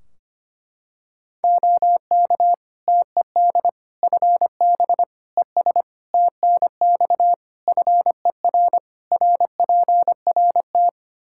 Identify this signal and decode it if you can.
Morse code, heard as a keyed tone, about 25 wpm, 700 Hz. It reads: OK TED FB ES TNX FER RPRT
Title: Practice QSO - Exchange 2 - Info - 25wpm